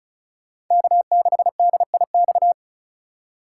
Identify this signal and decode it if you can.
Morse code, heard as a keyed tone, about 34 wpm, 700 Hz. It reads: K6DIX
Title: USA Call Signs - 5 Characters - 2x Character Spacing - 35wpm